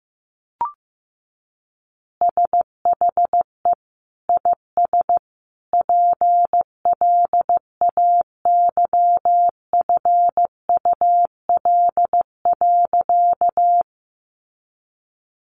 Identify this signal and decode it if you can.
Morse code, heard as a keyed tone, about 15 wpm, 700 Hz. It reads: SHE IS PLAYFUL.
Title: Easy Phrases 15wpm